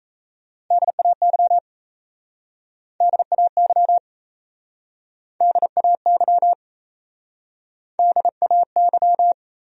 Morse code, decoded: DAY DAY DAY DAY

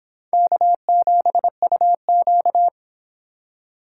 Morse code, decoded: K7UQ

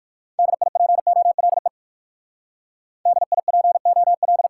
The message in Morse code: DIPOLE DIPO L

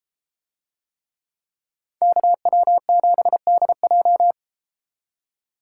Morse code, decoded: KW7DJ

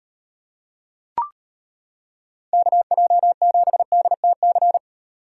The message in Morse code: KJ7DTC

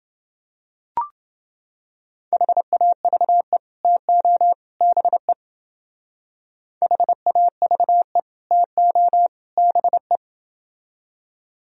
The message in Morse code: HAVE TO BE HAVE TO BE

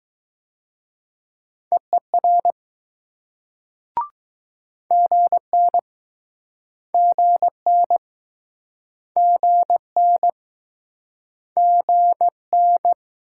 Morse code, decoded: EER GN GN GN GN

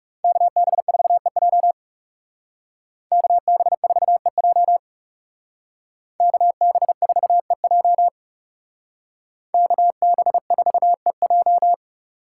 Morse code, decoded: KB4EJ KB4EJ KB4EJ KB4EJ

